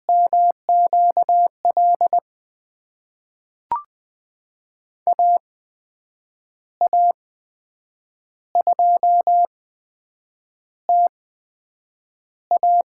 Morse code, heard as a keyed tone, about 20 wpm, spaced as for 7 wpm, 700 Hz. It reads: MQL AA2TA
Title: USA Call Signs - 8x Character Spacing - 20wpm